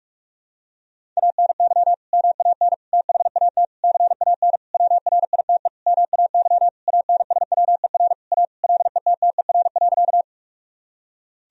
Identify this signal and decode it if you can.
Morse code, heard as a keyed tone, about 47 wpm, 700 Hz. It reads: ANY MAN THAT CAN WRITE MAY ANSWER A LETTER.
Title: Sentences from Top 1000 Words - Review Part 1 - 45wpm